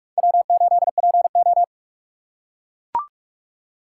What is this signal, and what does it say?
Morse code, heard as a keyed tone, about 44 wpm, 700 Hz. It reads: W8PO